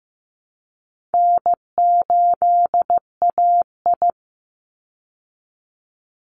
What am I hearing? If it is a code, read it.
Morse code, heard as a keyed tone, about 15 wpm, 700 Hz. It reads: N8AI
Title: Easy Call Signs Repeated 2x - 15wpm